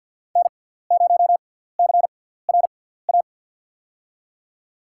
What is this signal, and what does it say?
Morse code, heard as a keyed tone, about 52 wpm, spaced as for 17 wpm, 700 Hz. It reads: N0CRA
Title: USA Call Signs - 5 Characters - 6x Character Spacing - 50wpm